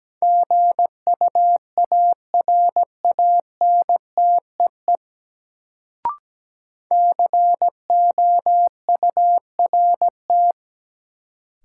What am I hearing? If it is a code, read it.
Morse code, heard as a keyed tone, about 17 wpm, 700 Hz. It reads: GUARANTEE COURT
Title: Sentences from Top 1600 Words - New Word Drill - 17wpm